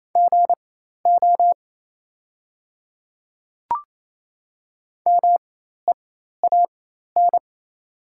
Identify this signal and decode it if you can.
Morse code, heard as a keyed tone, about 28 wpm, spaced as for 13 wpm, 700 Hz. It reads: GO MEAN
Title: 4x Character Spacing - Top 300 Words - 28wpm